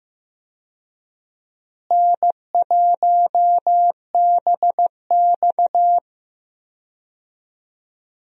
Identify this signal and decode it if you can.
Morse code, heard as a keyed tone, about 15 wpm, 700 Hz. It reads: N1BX
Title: USA Call Signs - 4 Characters - Repeated 3x - 15wpm